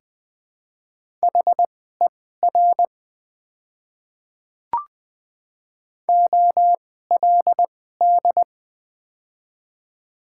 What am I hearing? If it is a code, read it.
Morse code, heard as a keyed tone, about 21 wpm, spaced as for 15 wpm, 700 Hz. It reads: HER OLD
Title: 2x Character Spacing - Top 300 Words - 20wpm